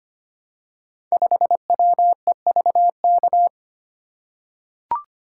Morse code, heard as a keyed tone, about 25 wpm, 700 Hz. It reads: 5WEVK
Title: Code Groups - 5 Letter:Numbers - 25wpm